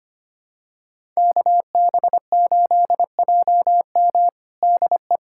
Morse code, heard as a keyed tone, about 25 wpm, 700 Hz. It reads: KB8JM DE
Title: USA Callsign DE Callsign - 5 Characters - 25wpm